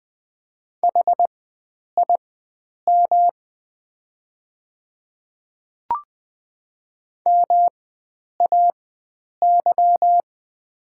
Morse code, heard as a keyed tone, about 20 wpm, spaced as for 9 wpm, 700 Hz. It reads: HIM MAY
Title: 4x Character Spacing - Top 100 Words - 20wpm